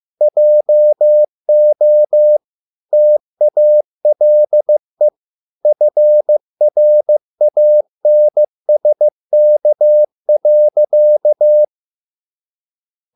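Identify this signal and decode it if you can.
Morse code, heard as a keyed tone, about 15 wpm, 600 Hz. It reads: JO TALE FRANSK.
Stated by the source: Morse code at a slow 15 wpm from CW med Gnister.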